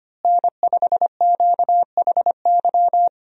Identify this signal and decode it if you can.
Morse code, heard as a keyed tone, about 25 wpm, 700 Hz. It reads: N5QHY
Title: USA Call Signs - 5 Characters - Repeated 3x - 25wpm